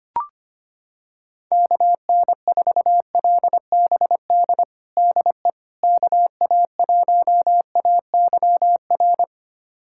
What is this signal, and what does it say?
Morse code, heard as a keyed tone, about 25 wpm, 700 Hz. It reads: KN4LBD DE KA1AYR
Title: USA Callsign DE Callsign - 6 Characters - 25wpm